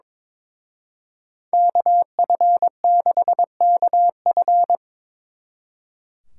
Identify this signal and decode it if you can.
Morse code, heard as a keyed tone, about 26 wpm, 700 Hz. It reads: KF6KF
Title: Speed Racing - USA 2x2 Call Signs 22wpm